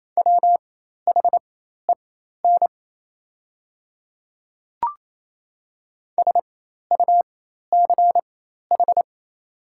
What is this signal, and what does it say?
Morse code, heard as a keyed tone, about 27 wpm, spaced as for 13 wpm, 700 Hz. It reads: WHEN SUCH
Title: 4x Character Spacing - Top 200 Words - 28wpm